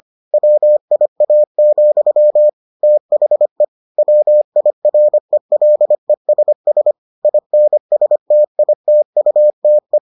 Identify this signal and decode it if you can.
Morse code, heard as a keyed tone, about 25 wpm, 600 Hz. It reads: WIA, THE WIRELESS INSTITUTE